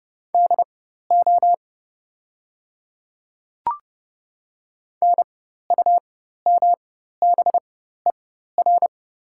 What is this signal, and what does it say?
Morse code, heard as a keyed tone, about 30 wpm, spaced as for 14 wpm, 690 Hz. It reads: DO NUMBER
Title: 4x Character Spacing - Top 100 Words - 30wpm